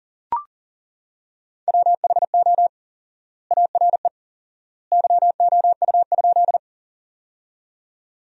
Morse code, heard as a keyed tone, about 40 wpm, 700 Hz. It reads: WHO ARE YOU?
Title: 4x Word-Spacing - Sentences from Top 100 Words - 40wpm